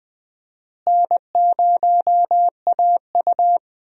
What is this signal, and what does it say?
Morse code, heard as a keyed tone, about 21 wpm, 700 Hz. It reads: N0AU
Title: USA Call Signs - 4 Characters - Repeated 3x - 20wpm